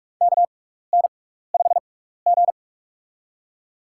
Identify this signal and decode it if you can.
Morse code, heard as a keyed tone, about 47 wpm, spaced as for 16 wpm, 700 Hz. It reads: KN5G
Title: USA Call Signs - 4 Characters - 6x Character Spacing - 45wpm